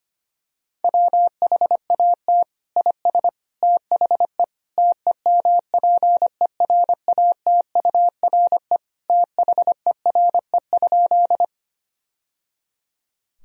Morse code, heard as a keyed tone, about 25 wpm, 700 Hz. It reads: WHAT IS THE TEMPERATURE THERE?
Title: Sentences from Top 1000 Words - Review Part 4 - 25wpm